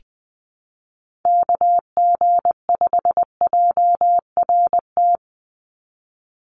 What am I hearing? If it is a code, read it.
Morse code, heard as a keyed tone, about 20 wpm, 700 Hz. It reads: KG5JRT
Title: USA Call Signs 20wpm